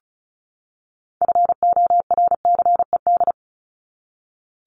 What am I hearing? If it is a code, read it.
Morse code, heard as a keyed tone, about 34 wpm, 700 Hz. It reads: FORCED
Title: ED Suffix - Top 100 Words with Base Words 35wpm